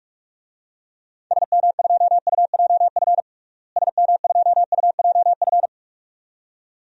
Morse code, heard as a keyed tone, about 46 wpm, 700 Hz. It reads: SM2UJF SM2UJF
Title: DXCC-100 Call Signs - 6 Characters - Repeated 2x - 45wpm